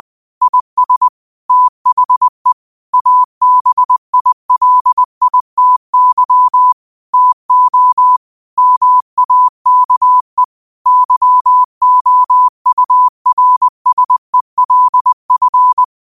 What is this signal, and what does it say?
Morse code, heard as a keyed tone, about 20 wpm, 1000 Hz. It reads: IS THE ABILITY TO MAKE YOURSELF
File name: qotdfiles / 20wpm